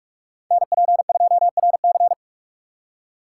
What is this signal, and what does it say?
Morse code, heard as a keyed tone, about 46 wpm, 700 Hz. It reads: NP2RC
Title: USA Call Signs - 4x Character Spacing - 45wpm